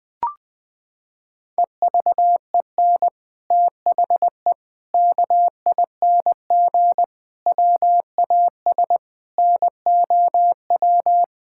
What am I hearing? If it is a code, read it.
Morse code, heard as a keyed tone, about 20 wpm, 700 Hz. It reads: EVEN THE KING WAS NOW
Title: Sentences from Top 1000 Words - Review Part 6 - 20wpm